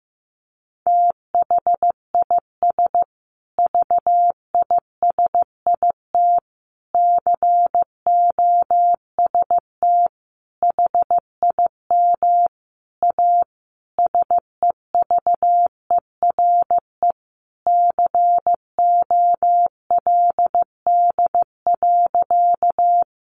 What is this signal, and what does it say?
Morse code, heard as a keyed tone, about 15 wpm, 700 Hz. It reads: THIS VISIT COST HIM A SEVERE COLD.
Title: Sentences from Top 1400 Words 15wpm